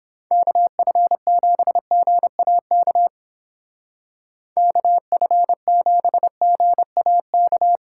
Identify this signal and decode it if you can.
Morse code, heard as a keyed tone, about 29 wpm, 700 Hz. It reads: KF7GAK KF7GAK